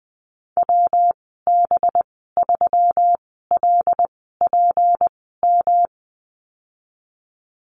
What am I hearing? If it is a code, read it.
Morse code, heard as a keyed tone, about 19 wpm, spaced as for 14 wpm, 700 Hz. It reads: WB3LPM
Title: 186 Lesson 30g - Letter b - Call Signs 20wpm